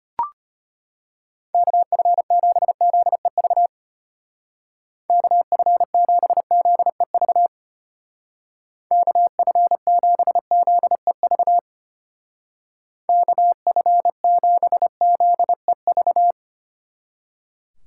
Morse code, heard as a keyed tone, about 31 wpm, 700 Hz. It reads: KF7ZEV KF7ZEV KF7ZEV KF7ZEV